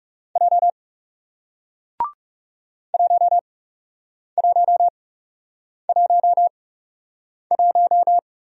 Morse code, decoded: J 1111

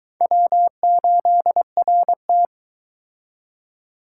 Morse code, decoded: W8RT

